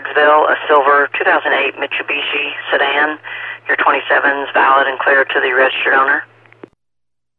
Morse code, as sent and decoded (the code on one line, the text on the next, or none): none